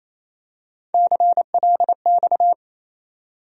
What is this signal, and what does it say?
Morse code, heard as a keyed tone, about 28 wpm, 700 Hz. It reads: CLX
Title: USA Call Sign Postfixes - 3 Letters 28wpm